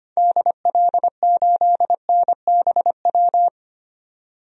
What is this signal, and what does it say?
Morse code, heard as a keyed tone, about 25 wpm, 700 Hz. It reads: DL8NBW
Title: DXCC-100 Call Signs - 6 Characters - 25wpm